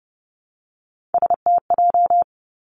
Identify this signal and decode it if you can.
Morse code, heard as a keyed tone, about 30 wpm, 700 Hz. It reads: STJ
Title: Three Challenging Letter-Number Combinations 30wpm